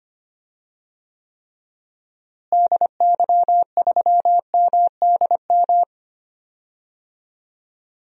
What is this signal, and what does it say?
Morse code, heard as a keyed tone, about 25 wpm, 700 Hz. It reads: DY3MDM
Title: DXCC-100 Call Signs - 6 Characters - Repeated 2x - 25wpm